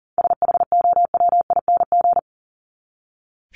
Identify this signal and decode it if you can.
Morse code, heard as a keyed tone, about 40 wpm, 700 Hz. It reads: SHOWING
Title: ING Suffix - Top 100 Words with Base Words 40wpm